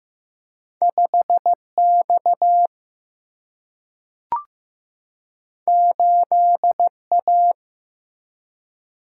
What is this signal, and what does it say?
Morse code, heard as a keyed tone, about 15 wpm, 700 Hz. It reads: 5X 8A
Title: Single-Letter Single-Number 15wpm